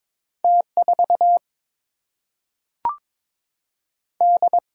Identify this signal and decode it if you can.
Morse code, heard as a keyed tone, about 22 wpm, 700 Hz. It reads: T 4 D
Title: Single-Letter Single-Number 22wpm